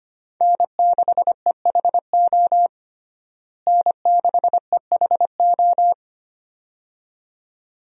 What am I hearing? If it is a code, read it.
Morse code, heard as a keyed tone, about 25 wpm, 700 Hz. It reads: N6EHO N6EHO
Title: USA Call Signs - 5 Characters - Repeated 2x - 25wpm